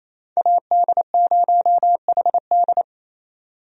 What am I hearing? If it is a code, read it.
Morse code, heard as a keyed tone, about 28 wpm, 700 Hz. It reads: AD0HD
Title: USA Call Signs - 5 Characters - 28wpm